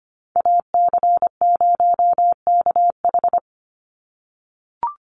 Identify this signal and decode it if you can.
Morse code, heard as a keyed tone, about 25 wpm, 700 Hz. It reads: AC0KH
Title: USA Call Signs 25wpm